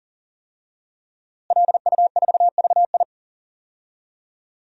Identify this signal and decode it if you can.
Morse code, heard as a keyed tone, about 40 wpm, 700 Hz. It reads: LU4VI
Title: DXCC-100 Call Signs - 5 Characters - Repeated 2x - 40wpm